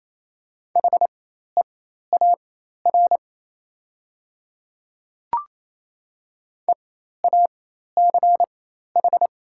Morse code, heard as a keyed tone, about 28 wpm, spaced as for 13 wpm, 690 Hz. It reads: HEAR EACH
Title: 4x Character Spacing - Top 500 Words - 28wpm